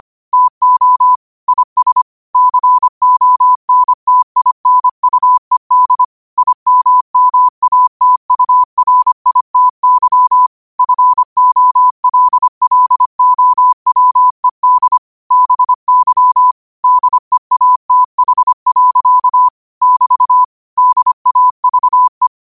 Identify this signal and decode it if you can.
Morse code, heard as a keyed tone, about 25 wpm, 1000 Hz. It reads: TO IS CONTINUED IMMATURITY FOLLOWED BY DEATH. <BT> DAVE
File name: qotdfiles / 25wpm